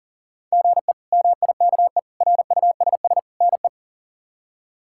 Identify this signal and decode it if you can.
Morse code, heard as a keyed tone, about 40 wpm, 690 Hz. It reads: GE MIKE RUSS NE
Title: K1USN SST QSO Practice - 40wpm